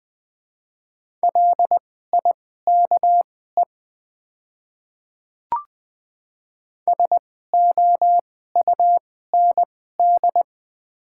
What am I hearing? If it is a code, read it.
Morse code, heard as a keyed tone, about 20 wpm, spaced as for 15 wpm, 700 Hz. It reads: LIKE SOUND
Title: 2x Character Spacing - Top 100 Words - 20wpm